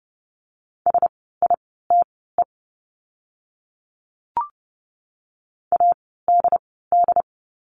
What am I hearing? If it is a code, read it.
Morse code, heard as a keyed tone, about 30 wpm, spaced as for 17 wpm, 700 Hz. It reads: SITE ADD
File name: Lesson 17f - Letter u - Words 30wpm